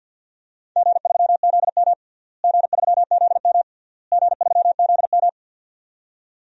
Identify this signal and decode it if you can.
Morse code, heard as a keyed tone, about 46 wpm, 700 Hz. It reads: G3ZM G3ZM G3ZM
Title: DXCC-100 Call Signs - 4 Characters - Repeated 3x - 50wpm